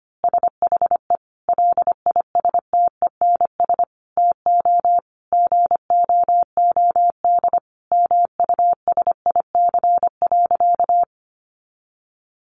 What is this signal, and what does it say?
Morse code, heard as a keyed tone, about 25 wpm, 700 Hz. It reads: SHE LISTENS TO GOOD MUSIC.
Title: Easy Phrases 25wpm